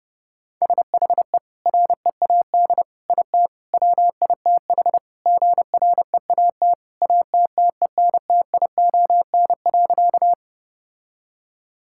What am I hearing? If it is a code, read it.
Morse code, heard as a keyed tone, about 30 wpm, 700 Hz. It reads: SHE READ IT WITH GREAT ATTENTION.
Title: Sentences from Top 1000 Words - Review Part 6 - 30wpm